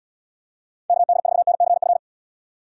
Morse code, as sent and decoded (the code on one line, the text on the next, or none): -.- -.. ....- -. -.-. ..-.
KD4NCF